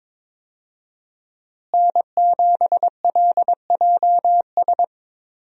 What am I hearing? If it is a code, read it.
Morse code, heard as a keyed tone, about 22 wpm, 700 Hz. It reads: N7LJS
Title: USA Call Signs - 5 Characters - Repeated 2x - 22wpm